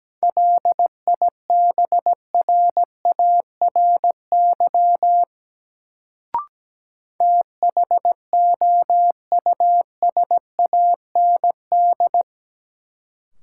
Morse code, heard as a keyed tone, about 17 wpm, 700 Hz. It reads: LIBRARY THOUSAND
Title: Sentences from Top 600 Words - New Word Drill - 17wpm